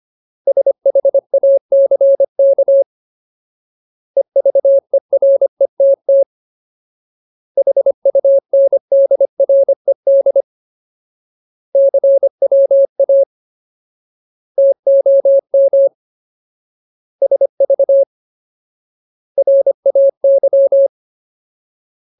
Morse code, heard as a keyed tone, about 25 wpm, 550 Hz. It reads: SHACK EVERETT HUNDRED CWA TOM SV RAY
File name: Ultimate Word List / 25WPM Ultimate